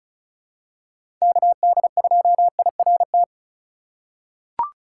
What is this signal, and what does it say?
Morse code, heard as a keyed tone, about 36 wpm, 700 Hz. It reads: KD2IRT
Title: USA Call Signs - 4x Character Spacing - 35wpm